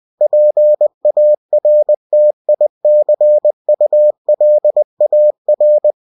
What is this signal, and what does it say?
Morse code, heard as a keyed tone, about 20 wpm, 600 Hz. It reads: PARTICULAR
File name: cw-books / winnie-the-pooh / winnie-20wpm-page-0024